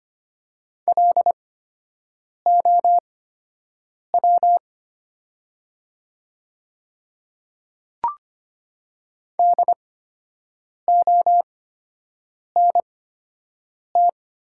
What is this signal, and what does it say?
Morse code, heard as a keyed tone, about 25 wpm, spaced as for 7 wpm, 700 Hz. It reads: LOW DONT
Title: 8x Character Spacing - Top 300 Words - 25wpm